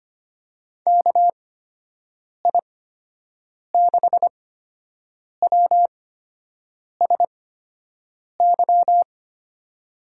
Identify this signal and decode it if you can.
Morse code, heard as a keyed tone, about 25 wpm, spaced as for 7 wpm, 700 Hz. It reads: KI6WSY
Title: USA Call Signs - 6 Characters - 8x Character Spacing - 25wpm